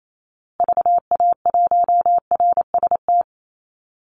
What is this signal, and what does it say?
Morse code, heard as a keyed tone, about 28 wpm, 700 Hz. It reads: VA1RST
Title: Call Signs Repeated 2x - 28wpm